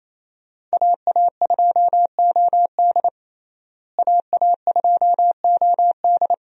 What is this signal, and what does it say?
Morse code, heard as a keyed tone, about 27 wpm, 700 Hz. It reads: AA2OD AA2OD
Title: USA Call Signs - 5 Characters - Repeated 3x - 28wpm